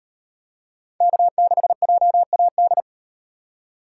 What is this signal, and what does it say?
Morse code, heard as a keyed tone, about 38 wpm, 700 Hz. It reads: K6JAD